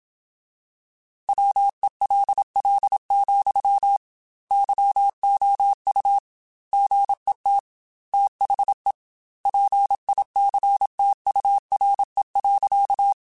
Morse code, heard as a keyed tone, about 27 wpm, 800 Hz. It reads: WELL, YOU GET THE PICTURE.
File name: Technical Subjects / Technical Subjects-mp3s / AND Gate @25